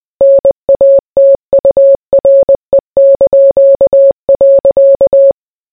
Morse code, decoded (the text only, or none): NATURE).